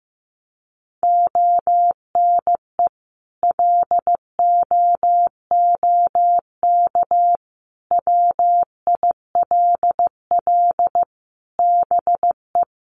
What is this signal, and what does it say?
Morse code, heard as a keyed tone, about 15 wpm, 700 Hz. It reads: ONE LOOK WILL BE
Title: Sentences from Top 1400 Words 15wpm